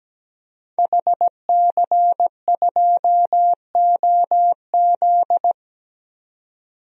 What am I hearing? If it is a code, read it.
Morse code, heard as a keyed tone, about 17 wpm, 700 Hz. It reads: HC2OZ
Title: DXCC-100 Call Signs - 5 Characters - 17wpm